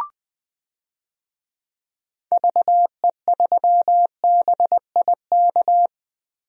Morse code, heard as a keyed tone, about 20 wpm, 700 Hz. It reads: VE3BIK
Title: DXCC-100 Call Signs - 6 Characters - Repeated 3x - 20wpm